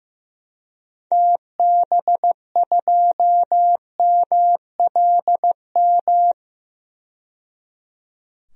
Morse code, heard as a keyed tone, about 15 wpm, 700 Hz. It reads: TB2MLM
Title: DXCC-100 Call Signs - 6 Characters - 15wpm